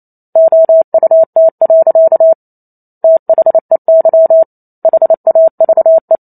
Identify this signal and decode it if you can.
Morse code, heard as a keyed tone, about 28 wpm, 650 Hz. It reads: OUT. THEY HAVE